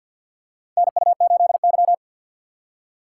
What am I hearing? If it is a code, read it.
Morse code, heard as a keyed tone, about 51 wpm, 700 Hz. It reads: NU8Y
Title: USA Call Signs - 4 Characters - 3x Character Spacing - 50wpm